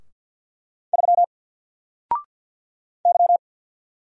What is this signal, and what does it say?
Morse code, heard as a keyed tone, about 50 wpm, 700 Hz. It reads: 3 Y
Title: Single Character (Letter:Number:Punctuation:Prosign) 50wpm